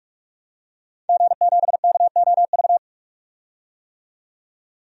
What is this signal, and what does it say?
Morse code, heard as a keyed tone, about 46 wpm, 700 Hz. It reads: G7KOV
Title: DXCC-100 Call Signs - 5 Characters - 45wpm